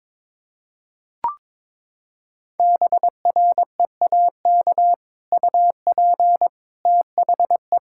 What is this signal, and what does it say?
Morse code, heard as a keyed tone, about 21 wpm, 700 Hz. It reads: BREAK UP THE
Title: Sentences from Top 1600 Words 22wpm